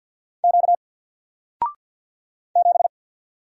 Morse code, heard as a keyed tone, about 50 wpm, 700 Hz. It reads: Q 7
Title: Single Letter-Number 50wpm